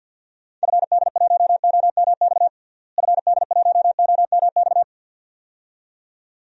Morse code, heard as a keyed tone, about 51 wpm, 700 Hz. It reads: FD1OMX FD1OMX
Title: DXCC-100 Call Signs - 6 Characters - Repeated 2x - 50wpm